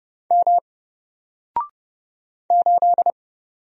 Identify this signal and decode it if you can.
Morse code, heard as a keyed tone, about 29 wpm, 700 Hz. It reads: M 8
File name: Single-Letter-Number-30wpm